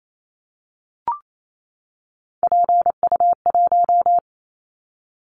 Morse code, heard as a keyed tone, about 28 wpm, 700 Hz. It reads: PU1
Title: Callsign Prefixes for Top 100 DXCC - 28wpm